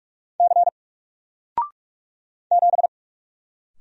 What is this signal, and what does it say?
Morse code, heard as a keyed tone, about 44 wpm, 700 Hz. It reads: C 7
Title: Single Character (Letter:Number:Punctuation:Prosign) 45wpm